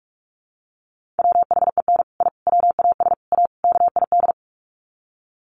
Morse code, heard as a keyed tone, about 44 wpm, 700 Hz. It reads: WHEN I WAS A KID
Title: Sets of 5 Words 45wpm